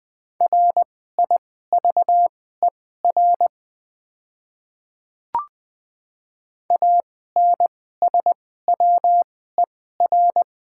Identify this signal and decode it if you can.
Morse code, heard as a keyed tone, about 20 wpm, spaced as for 14 wpm, 690 Hz. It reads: RIVER ANSWER
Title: 2x Character Spacing - Top 300 Words - 20wpm